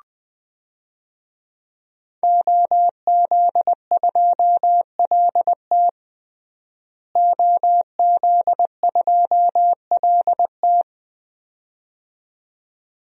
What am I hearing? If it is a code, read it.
Morse code, heard as a keyed tone, about 20 wpm, 700 Hz. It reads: OZ2LT OZ2LT
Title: DXCC-100 Call Signs - 5 Characters - Repeated 2x - 20wpm